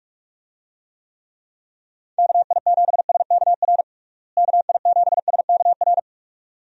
Morse code, heard as a keyed tone, about 41 wpm, 700 Hz. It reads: KI7SKR KI7SKR
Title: USA Call Signs - 6 Characters - Repeated 2x - 45wpm